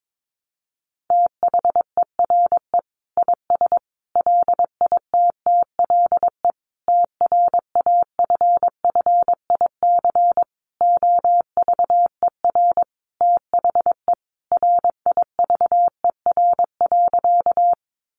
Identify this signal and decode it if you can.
Morse code, heard as a keyed tone, about 22 wpm, 700 Hz. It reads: THERE IS LITTLE TRAFFIC OVER THE RIVER.
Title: Sentences from Top 1300 Words 22wpm